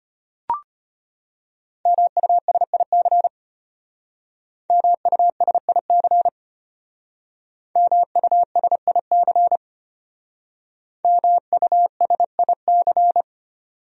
Morse code, heard as a keyed tone, about 32 wpm, 700 Hz. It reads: MUSIC MUSIC MUSIC MUSIC